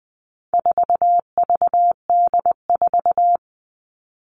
Morse code, heard as a keyed tone, about 20 wpm, 700 Hz. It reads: 4VD4
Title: All Cut Numbers - 4-Digit 20wpm